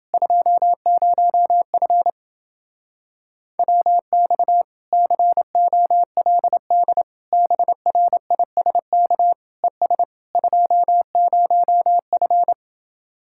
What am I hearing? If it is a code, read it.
Morse code, heard as a keyed tone, about 29 wpm, 700 Hz. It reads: 20F WX COLD BRISK ES 20F